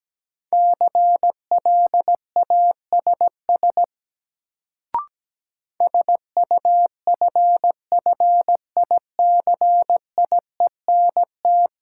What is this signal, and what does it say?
Morse code, heard as a keyed tone, about 17 wpm, 700 Hz. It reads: CLASS SUFFICIENT
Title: Sentences from Top 1400 Words - New Word Drill Alt - 17wpm